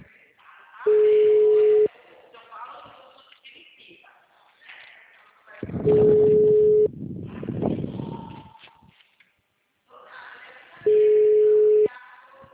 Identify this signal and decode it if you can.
Ringback tone (Europe, 425 Hz), with speech.